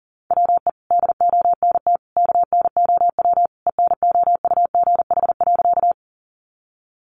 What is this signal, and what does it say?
Morse code, heard as a keyed tone, about 40 wpm, 700 Hz. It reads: WE DONT KNOW ENOUGH.
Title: Sentences from Top 400 Words 40wpm